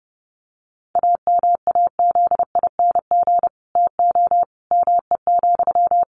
Morse code, heard as a keyed tone, about 30 wpm, 700 Hz. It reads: AMAZING TO ME,
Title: Sentences from Top 1500 Words 30wpm